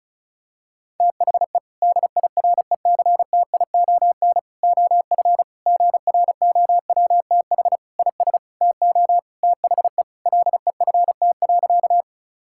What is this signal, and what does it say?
Morse code, heard as a keyed tone, about 36 wpm, 700 Hz. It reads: THE DIRECTION OF GROWTH IS TO THE LEFT.
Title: Sentences from Top 1000 Words - Review Part 5 - 35wpm